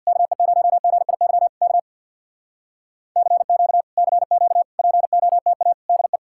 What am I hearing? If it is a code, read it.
Morse code, heard as a keyed tone, about 63 wpm, 700 Hz. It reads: KE0GIY K CQ CQ POTA DE